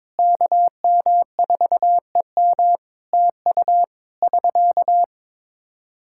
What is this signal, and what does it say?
Morse code, heard as a keyed tone, about 22 wpm, 700 Hz. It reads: KM4EM TU <SK>
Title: Practice QSO - Exchange 4 - Ending - 22wpm